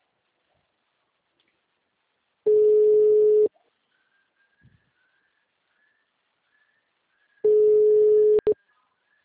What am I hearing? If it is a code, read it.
Ringback tone (Europe, 425 Hz), with speech.